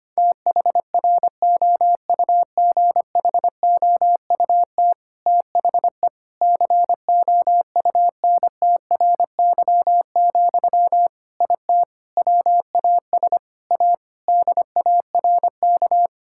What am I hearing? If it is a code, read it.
Morse code, heard as a keyed tone, about 25 wpm, 700 Hz. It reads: THROUGHOUT THE COUNTRY, IT WAS A DARK
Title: Sentences from Top 1000 Words - Review Part 5 - 25wpm